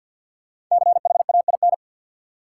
Morse code, heard as a keyed tone, about 53 wpm, 700 Hz. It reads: CHAIN
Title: Sentences from Top 1500 Words - New Word Drill - 50wpm